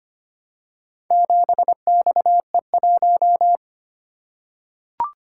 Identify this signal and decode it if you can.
Morse code, heard as a keyed tone, about 25 wpm, 700 Hz. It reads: 7XE1
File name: practice / 25wpm / Four Letter-Number Combinations - 25wpm